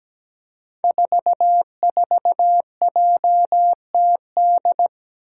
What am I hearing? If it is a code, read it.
Morse code, heard as a keyed tone, about 17 wpm, 700 Hz. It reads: 44JTD
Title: Five Challenging Letter-Number Combinations 17wpm